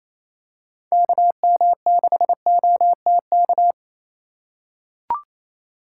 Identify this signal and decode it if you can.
Morse code, heard as a keyed tone, about 28 wpm, 700 Hz. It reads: KM6OTK
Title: USA Call Signs - 4x Character Spacing - 28wpm